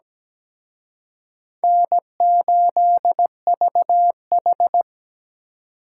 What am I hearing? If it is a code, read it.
Morse code, heard as a keyed tone, about 20 wpm, 700 Hz. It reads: N8VH
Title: Speed Racing - USA Call Signs 17wpm